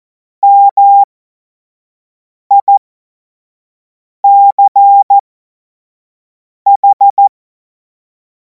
Morse code, heard as a keyed tone, about 14 wpm, spaced as for 5 wpm, 800 Hz. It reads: MICH
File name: Morsekurs / Witze / Witz 24. März 2026 25 F70